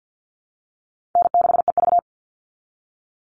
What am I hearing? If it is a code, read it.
Morse code, heard as a keyed tone, about 50 wpm, 700 Hz. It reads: N6EV
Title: Call Signs Repeated 3x - 50wpm